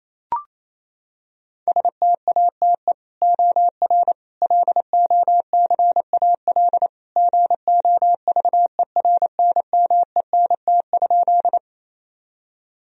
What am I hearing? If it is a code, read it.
Morse code, heard as a keyed tone, about 28 wpm, 700 Hz. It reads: STATE OR LOCAL GOVERNMENT?
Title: Sentences from Top 1000 Words - Review Part 3 - 28wpm